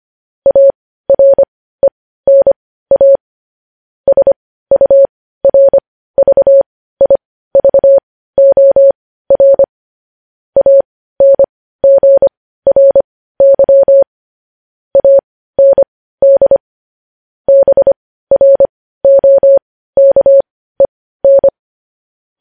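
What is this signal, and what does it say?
Morse code, heard as a keyed tone, about 25 wpm, spaced as for 15 wpm, 550 Hz. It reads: ARENA SURVIVOR ANGRY AND BROKEN
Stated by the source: News headlines sent in Morse code at 15 wpm.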